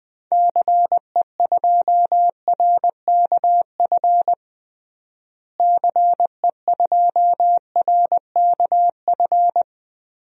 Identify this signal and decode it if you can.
Morse code, heard as a keyed tone, about 20 wpm, 700 Hz. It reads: CE2RKF CE2RKF
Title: DXCC-100 Call Signs - 6 Characters - Repeated 2x - 20wpm